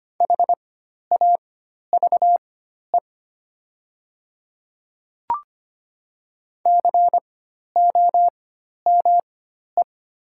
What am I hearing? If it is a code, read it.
Morse code, heard as a keyed tone, about 25 wpm, spaced as for 12 wpm, 700 Hz. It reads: HAVE COME
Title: 4x Character Spacing - Top 100 Words - 25wpm